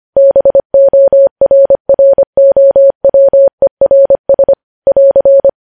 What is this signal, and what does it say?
Morse code, heard as a keyed tone, about 25 wpm, 550 Hz. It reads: BORROWERS <AR>
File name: podcast / News headlines-1x-25x25